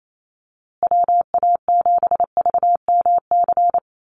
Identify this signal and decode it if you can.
Morse code, heard as a keyed tone, about 28 wpm, 700 Hz. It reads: WA7VMC
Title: USA Call Signs Repeated 3x 28wpm